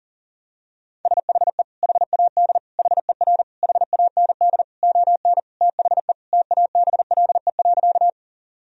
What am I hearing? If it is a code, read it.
Morse code, heard as a keyed tone, about 40 wpm, 700 Hz. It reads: SHE HAD HER HAND ON THE TABLE.
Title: Sentences from Top 1000 Words - Review Part 2 - 40wpm